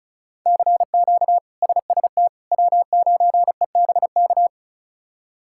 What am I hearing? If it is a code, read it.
Morse code, heard as a keyed tone, about 34 wpm, 700 Hz. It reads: CQ SST W9EBK
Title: K1USN SST QSO Practice - 35wpm